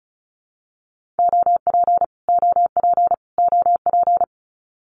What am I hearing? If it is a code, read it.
Morse code, heard as a keyed tone, about 34 wpm, 700 Hz. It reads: OP OP OP
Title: Abbreviations Repeated 3x - 35wpm